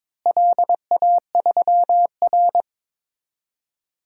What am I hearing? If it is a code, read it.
Morse code, heard as a keyed tone, about 22 wpm, 700 Hz. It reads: LA3R
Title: DXCC-100 Call Signs - 4 Characters - Repeated 3x - 22wpm